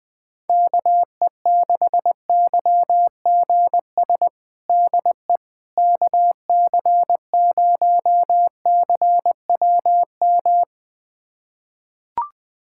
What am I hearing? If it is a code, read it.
Morse code, heard as a keyed tone, about 21 wpm, 700 Hz. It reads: KE6YGS DE KC0CWM
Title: USA Callsign DE Callsign 20wpm